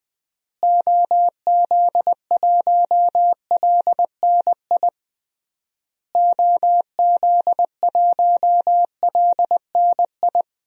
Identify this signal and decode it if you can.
Morse code, heard as a keyed tone, about 20 wpm, 700 Hz. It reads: OZ1LNI OZ1LNI
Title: DXCC-100 Call Signs - 6 Characters - Repeated 3x - 20wpm